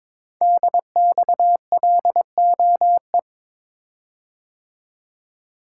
Morse code, heard as a keyed tone, about 22 wpm, 700 Hz. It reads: DXLOE
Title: Code Groups - 5 Letters - 22wpm